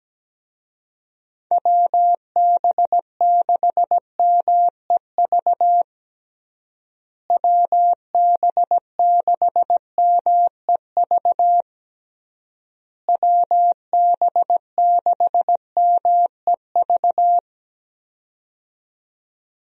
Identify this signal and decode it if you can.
Morse code, heard as a keyed tone, about 17 wpm, 700 Hz. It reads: WB6MEV WB6MEV WB6MEV
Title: USA Call Signs - 6 Characters - Repeated 3x - 17wpm